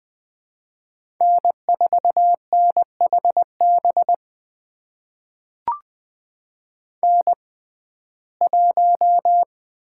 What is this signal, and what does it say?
Morse code, heard as a keyed tone, about 20 wpm, spaced as for 14 wpm, 700 Hz. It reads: N4NHB N1
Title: USA Call Signs - 6x Character Spacing - 20wpm